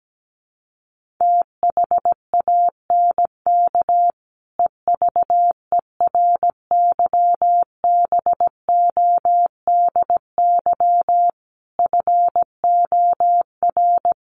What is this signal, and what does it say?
Morse code, heard as a keyed tone, about 17 wpm, 700 Hz. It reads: THANK EVERYBODY FOR
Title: Sentences from Top 1400 Words 17wpm